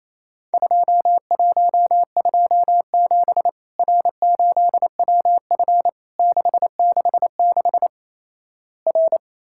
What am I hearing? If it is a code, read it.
Morse code, heard as a keyed tone, about 28 wpm, 700 Hz. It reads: 2127 R8WF 666 R
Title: Worked All Europe DX Contest - QTC Exchange - 28wpm